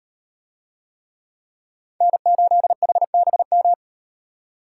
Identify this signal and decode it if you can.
Morse code, heard as a keyed tone, about 38 wpm, 700 Hz. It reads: N8HBM